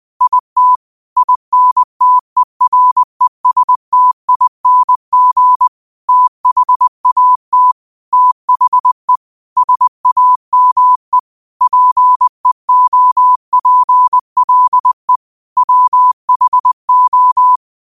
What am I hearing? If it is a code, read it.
Morse code, heard as a keyed tone, about 20 wpm, 1000 Hz. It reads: IT INTERESTING THAT THE SAME PEOPLE WHO
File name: 20wpm